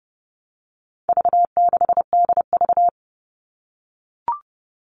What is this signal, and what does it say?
Morse code, heard as a keyed tone, about 30 wpm, 700 Hz. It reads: V6DV
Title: All Cut Numbers - 4-Digit 30wpm